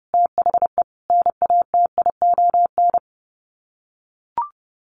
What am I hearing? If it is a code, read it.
Morse code, heard as a keyed tone, about 30 wpm, 700 Hz. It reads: THE NATION
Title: Sets of 3 Words 30wpm